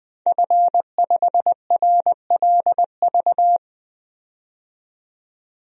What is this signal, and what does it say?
Morse code, heard as a keyed tone, about 20 wpm, 700 Hz. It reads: F5RLV
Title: DXCC-100 Call Signs - 5 Characters - 20wpm